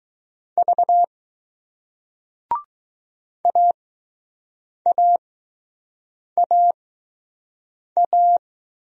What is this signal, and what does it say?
Morse code, heard as a keyed tone, about 20 wpm, spaced as for 6 wpm, 700 Hz. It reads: V AAAA